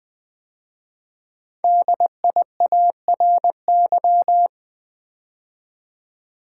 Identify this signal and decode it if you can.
Morse code, heard as a keyed tone, about 20 wpm, 700 Hz. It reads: DIARY
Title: 5 Letter Words - First 3 Letters Given - 20wpm